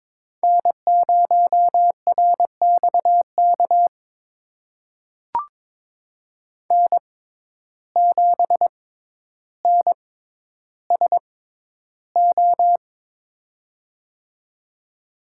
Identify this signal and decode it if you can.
Morse code, heard as a keyed tone, about 22 wpm, spaced as for 11 wpm, 700 Hz. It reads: N0RXK N7NSO
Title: USA Call Signs - 5 Characters - 6x Character Spacing - 22wpm